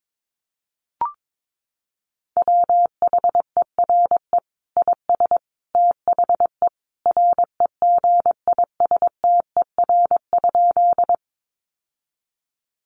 Morse code, heard as a keyed tone, about 22 wpm, 700 Hz. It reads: WHERE IS THE REGISTER?
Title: Sentences from Top 1200 Words 22wpm